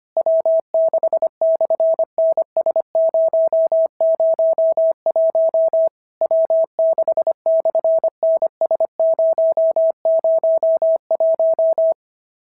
Morse code, heard as a keyed tone, about 25 wpm, 650 Hz. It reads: W6/NS001 W6/NS001
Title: SOTA QSO Practice 25wpm